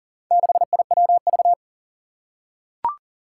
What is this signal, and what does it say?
Morse code, heard as a keyed tone, about 38 wpm, 700 Hz. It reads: 6IWV
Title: Four Challenging Letter-Number Combinations 40wpm